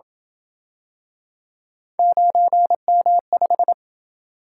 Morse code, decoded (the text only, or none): 9M5